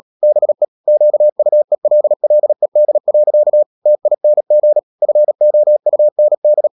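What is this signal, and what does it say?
Morse code, heard as a keyed tone, about 37 wpm, 600 Hz. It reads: BE QUELLED. TING FOUND
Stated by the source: Morse code practice at 35–39 wpm sending English text of Ken Follett's novel Never, from CW med Gnister.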